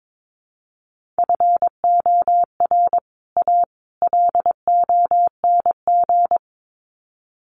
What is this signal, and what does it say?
Morse code, heard as a keyed tone, about 22 wpm, 700 Hz. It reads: FOR A LONG
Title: Sets of 3 Words 22wpm